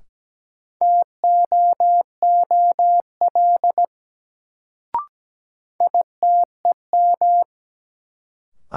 Morse code, heard as a keyed tone, about 17 wpm, 700 Hz. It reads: TOOL ITEM
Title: Sentences from Top 1000 Words - Review Part 6 - New Word Drill - 17wpm